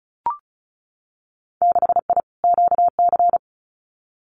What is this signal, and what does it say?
Morse code, heard as a keyed tone, about 36 wpm, 700 Hz. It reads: 6I QC
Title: Winter Field Day Exchanges Encore 35wpm